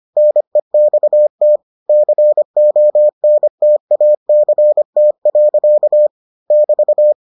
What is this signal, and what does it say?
Morse code, heard as a keyed tone, about 25 wpm, 600 Hz. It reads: NEXT CONTACT. <BT>